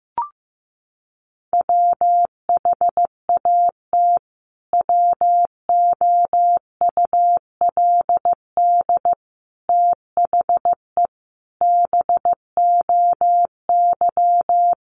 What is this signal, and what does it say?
Morse code, heard as a keyed tone, about 14 wpm, 700 Hz. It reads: WHAT WOULD THE BOY
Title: Sentences from Top 1300 Words 15wpm